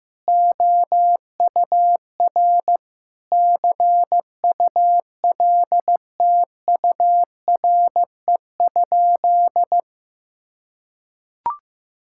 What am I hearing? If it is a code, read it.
Morse code, heard as a keyed tone, about 15 wpm, 700 Hz. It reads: OUR CULTURE?
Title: Sentences from Top 1000 Words - Review Part 6 - 15wpm